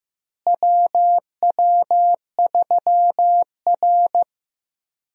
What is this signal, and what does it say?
Morse code, heard as a keyed tone, about 15 wpm, 700 Hz. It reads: WW3R
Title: USA Call Signs - 4 Characters - Repeated 2x - 15wpm